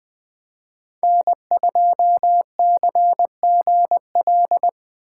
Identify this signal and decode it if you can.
Morse code, heard as a keyed tone, about 24 wpm, 700 Hz. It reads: N2CGL
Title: Speed Racing - USA 1x3 Call Signs 20wpm